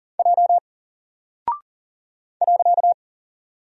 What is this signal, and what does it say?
Morse code, heard as a keyed tone, about 40 wpm, 700 Hz. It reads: J .
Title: Single Character (Letter:Number:Punctuation:Prosign) Alt 40wpm